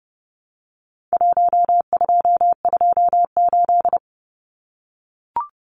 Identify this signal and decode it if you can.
Morse code, heard as a keyed tone, about 30 wpm, 700 Hz. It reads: 1228
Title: Numbers - Contest Style - 4-Digit 30wpm